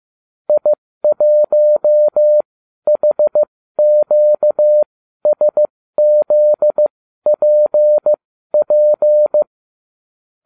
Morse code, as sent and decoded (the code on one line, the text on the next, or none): .. .---- .... --.- ... --.. .--. .--.
I1HQSZPP